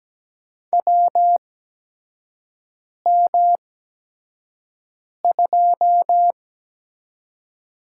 Morse code, heard as a keyed tone, about 17 wpm, spaced as for 5 wpm, 700 Hz. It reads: WM2
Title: USA Call Signs - 4 Characters - 8x Character Spacing - 17wpm